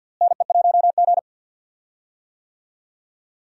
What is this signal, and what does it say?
Morse code, heard as a keyed tone, about 50 wpm, 700 Hz. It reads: NE1G
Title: USA Call Signs - 4 Characters - Repeated 2x - 50wpm